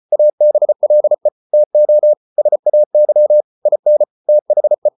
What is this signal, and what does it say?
Morse code, heard as a keyed tone, about 34 wpm, 600 Hz. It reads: ABLE TO SAY IN THE